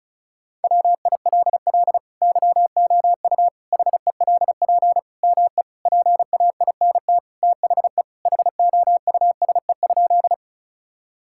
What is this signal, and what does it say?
Morse code, heard as a keyed tone, about 34 wpm, 690 Hz. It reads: WILL YOU HELP ME PAINT THE HOUSE?
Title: Sentences from Top 1000 Words - Review Part 6 - 35wpm